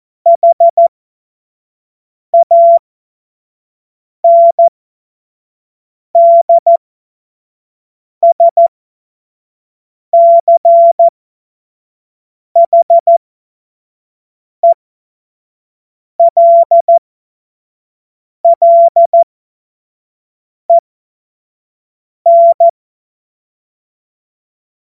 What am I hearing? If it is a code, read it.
Morse code, heard as a keyed tone, about 14 wpm, spaced as for 5 wpm, 680 Hz. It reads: HANDSCHELLEN
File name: Morsekurs / Witze / Witz 23. Mai 2023 25 F70